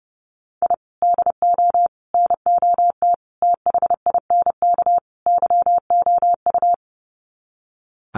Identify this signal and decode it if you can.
Morse code, heard as a keyed tone, about 30 wpm, 700 Hz. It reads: I DO NOT THINK YOU
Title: Sets of 5 Words 30wpm